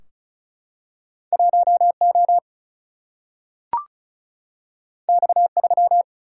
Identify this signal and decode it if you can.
Morse code, heard as a keyed tone, about 36 wpm, 700 Hz. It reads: 1O X3
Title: Single-Letter Single-Number 35wpm